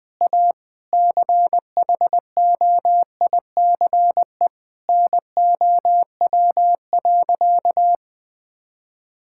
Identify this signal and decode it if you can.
Morse code, heard as a keyed tone, about 20 wpm, 700 Hz. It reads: A CHOICE NOW.
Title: Sentences with 5-Word Phrases - 20wpm